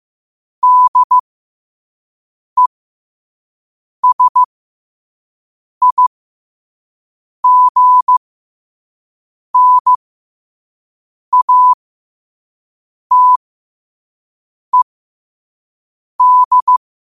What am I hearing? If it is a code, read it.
Morse code, heard as a keyed tone, about 15 wpm, spaced as for 5 wpm, 1000 Hz. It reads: DESIGNATED
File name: qotdfiles / 5wpm